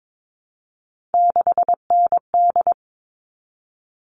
Morse code, heard as a keyed tone, about 22 wpm, 700 Hz. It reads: 6ND
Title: All Cut Numbers - 3-Digit 22wpm